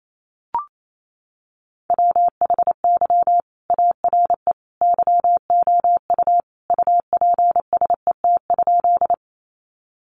Morse code, heard as a keyed tone, about 28 wpm, 700 Hz. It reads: WHY ARE YOU UPSET?
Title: Sentences from Top 800 Words 28wpm